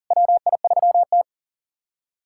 Morse code, heard as a keyed tone, about 40 wpm, 700 Hz. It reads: WI3T
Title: USA Callsign DE Callsign - 4 Characters - 40wpm